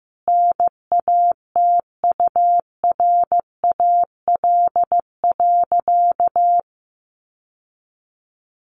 Morse code, heard as a keyed tone, about 15 wpm, 700 Hz. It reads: NATURAL.
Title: Sentences from Top 700 Words 15wpm